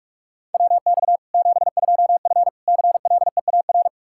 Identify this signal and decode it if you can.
Morse code, heard as a keyed tone, about 42 wpm, 700 Hz. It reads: WX 72F CLEAR
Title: Practice QSO - Exchange 2 - Info - 45wpm